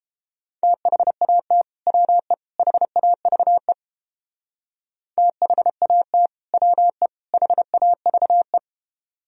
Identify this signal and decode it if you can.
Morse code, heard as a keyed tone, about 29 wpm, 700 Hz. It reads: THAT WE HAVE THAT WE HAVE